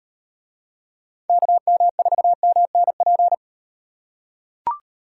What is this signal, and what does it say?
Morse code, heard as a keyed tone, about 38 wpm, 700 Hz. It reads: KM4MNP